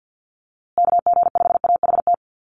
Morse code, heard as a keyed tone, about 50 wpm, 700 Hz. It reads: KG5AHT
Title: USA Call Signs Repeated 3x 50wpm